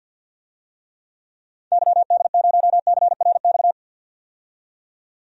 Morse code, heard as a keyed tone, about 49 wpm, 700 Hz. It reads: YD0CRX
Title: DXCC-100 Call Signs - 6 Characters - 50wpm